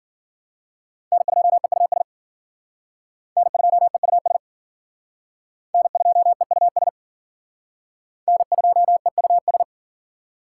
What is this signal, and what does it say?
Morse code, heard as a keyed tone, about 52 wpm, 700 Hz. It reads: N2EUS N2EUS N2EUS N2EUS